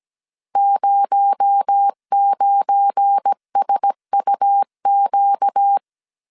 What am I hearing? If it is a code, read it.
Morse code, heard as a keyed tone, about 17 wpm, 800 Hz. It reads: 09SUQ
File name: CW / DET / 1 9 1500C S C 16wpm 2909